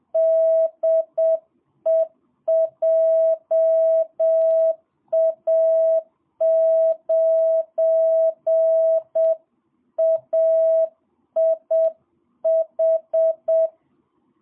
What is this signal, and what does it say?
Morse code, heard as a keyed tone, about 7 wpm, 660 Hz. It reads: DE JA9AIH
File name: jh8cbh / 2022cw